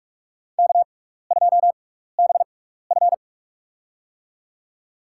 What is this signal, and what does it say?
Morse code, heard as a keyed tone, about 42 wpm, spaced as for 15 wpm, 700 Hz. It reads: K2BF
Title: USA Call Signs - 4 Characters - 6x Character Spacing - 45wpm